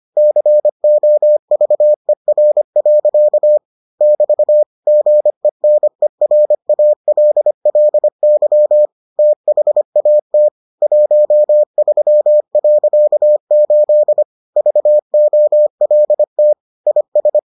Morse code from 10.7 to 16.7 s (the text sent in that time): 13.8 VOLT